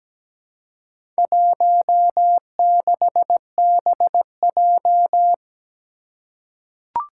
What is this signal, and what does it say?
Morse code, heard as a keyed tone, about 17 wpm, 700 Hz. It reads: 16BJ
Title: Four Challenging Letter-Number Combinations 17wpm